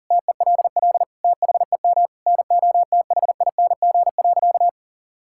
Morse code, heard as a keyed tone, about 40 wpm, 700 Hz. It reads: TELL THEM NOTHING.
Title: Sentences from Top 1000 Words - Review Part 6 - 40wpm